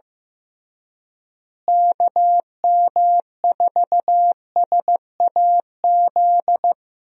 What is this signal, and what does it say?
Morse code, heard as a keyed tone, about 15 wpm, 700 Hz. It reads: KM4SAZ
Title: USA Call Signs - 6 Characters - Repeated 2x - 15wpm